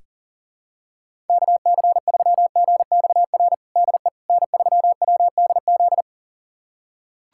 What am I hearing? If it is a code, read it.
Morse code, heard as a keyed tone, about 40 wpm, 700 Hz. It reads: KC3GXR DE N3WDZ
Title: USA Callsign DE Callsign 40wpm